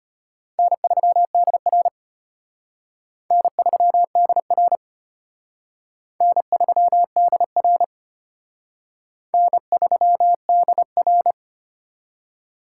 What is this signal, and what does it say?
Morse code, heard as a keyed tone, about 31 wpm, 700 Hz. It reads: N3DR N3DR N3DR N3DR
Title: Speed Racing - USA 1x2 Call Signs 25wpm